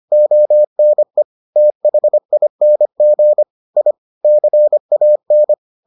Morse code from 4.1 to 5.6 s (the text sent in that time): CAN